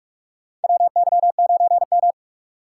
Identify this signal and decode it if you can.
Morse code, heard as a keyed tone, about 46 wpm, 700 Hz. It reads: WY9M
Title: USA Call Signs - 3x Character Spacing - 45wpm